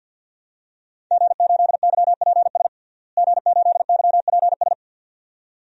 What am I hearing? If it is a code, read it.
Morse code, heard as a keyed tone, about 48 wpm, 700 Hz. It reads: G8YPS G8YPS
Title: DXCC-100 Call Signs - 5 Characters - Repeated 2x - 50wpm